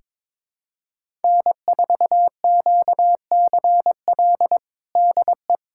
Morse code, heard as a keyed tone, about 22 wpm, 700 Hz. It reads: N4QCL DE
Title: USA Callsign DE Callsign - 5 Characters - 22wpm